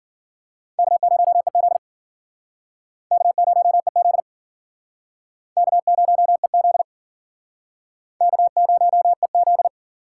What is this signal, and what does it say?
Morse code, heard as a keyed tone, about 52 wpm, 700 Hz. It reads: K0EZ K0EZ K0EZ K0EZ